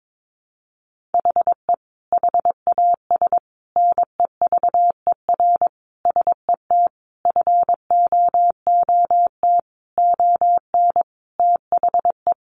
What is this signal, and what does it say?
Morse code, heard as a keyed tone, about 22 wpm, 700 Hz. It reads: HE HAS NEVER SET FOOT ON THE
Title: Sentences from Top 1500 Words 22wpm